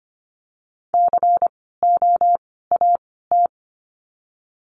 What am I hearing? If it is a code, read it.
Morse code, heard as a keyed tone, about 25 wpm, spaced as for 16 wpm, 700 Hz. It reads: COAT
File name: course / 25:10wpm / 094 Lesson 16f - Letter c - Words 25wpm